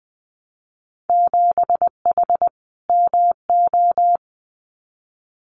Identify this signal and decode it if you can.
Morse code, heard as a keyed tone, about 21 wpm, 700 Hz. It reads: 7H MO
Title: Winter Field Day Exchanges Encore 20wpm